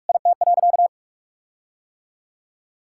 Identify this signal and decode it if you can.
Morse code, heard as a keyed tone, about 43 wpm, 700 Hz. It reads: IT.
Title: Sentences from Top 1600 Words 45wpm